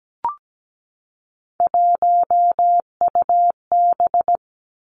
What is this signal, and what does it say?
Morse code, heard as a keyed tone, about 17 wpm, 700 Hz. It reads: 1UB
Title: Three Challenging Letter-Number Combinations 17wpm